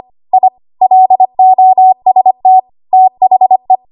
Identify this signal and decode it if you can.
Morse code, heard as a keyed tone, about 25 wpm, 750 Hz. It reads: I LOST THE
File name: Archive / 25 WPM / 25WPM